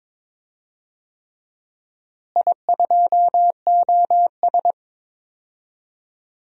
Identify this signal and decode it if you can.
Morse code, heard as a keyed tone, about 21 wpm, 700 Hz. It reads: I2OS
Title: DXCC-100 Call Signs - 4 Characters - Repeated 3x - 22wpm